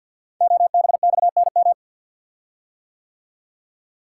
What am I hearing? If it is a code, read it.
Morse code, heard as a keyed tone, about 48 wpm, 700 Hz. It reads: OBXNM
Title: Code Groups - 5 Letters - 50wpm